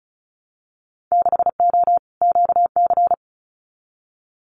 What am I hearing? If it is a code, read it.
Morse code, heard as a keyed tone, about 36 wpm, 700 Hz. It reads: 6O QC
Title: Winter Field Day Exchanges Encore 35wpm